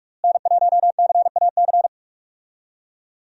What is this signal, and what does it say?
Morse code, heard as a keyed tone, about 47 wpm, 700 Hz. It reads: N1CAC
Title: USA Call Signs - 5 Characters - Repeated 2x - 45wpm